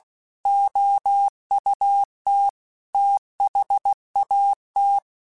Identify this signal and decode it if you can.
Morse code, heard as a keyed tone, about 16 wpm, 800 Hz. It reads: OUT THAT